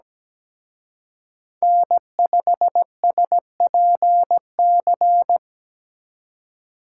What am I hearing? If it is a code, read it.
Morse code, heard as a keyed tone, about 20 wpm, 700 Hz. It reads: N5SPC
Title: Speed Racing - USA 1x3 Call Signs 17wpm